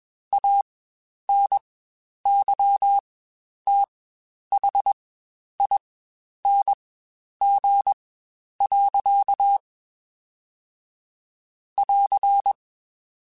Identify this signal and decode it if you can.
Morse code, heard as a keyed tone, about 21 wpm, spaced as for 10 wpm, 800 Hz. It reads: ANYTHING. <AR>
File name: Quotes Spaced / Quotes Spaced-mp3s / StandFor sp @20